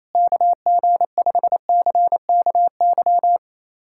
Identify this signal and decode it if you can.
Morse code, heard as a keyed tone, about 28 wpm, 700 Hz. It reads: KG5CKY
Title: USA Callsign DE Callsign - 6 Characters - 28wpm